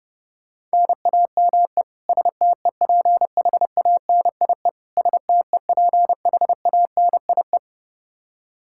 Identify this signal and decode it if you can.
Morse code, heard as a keyed tone, about 32 wpm, 700 Hz. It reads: NAME STEPHANIE STEPHANIE
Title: Practice QSO - Exchange 1 - Beginning - 30wpm